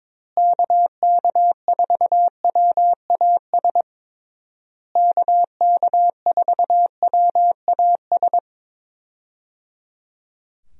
Morse code, heard as a keyed tone, about 21 wpm, 700 Hz. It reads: KK4WAS KK4WAS
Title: USA Call Signs - 6 Characters - Repeated 3x - 22wpm